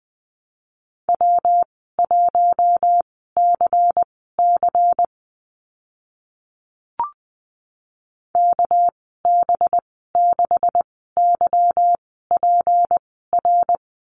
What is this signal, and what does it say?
Morse code, heard as a keyed tone, about 20 wpm, spaced as for 14 wpm, 700 Hz. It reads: W1CC KB6YPR
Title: 238 Lesson 38g - Letter q - Call Signs 20wpm